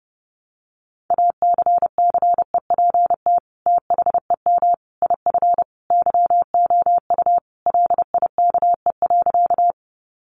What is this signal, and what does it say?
Morse code, heard as a keyed tone, about 30 wpm, 700 Hz. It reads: ACCEPT THEM IF YOU LIKE.
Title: Sentences from Top 1300 Words 30wpm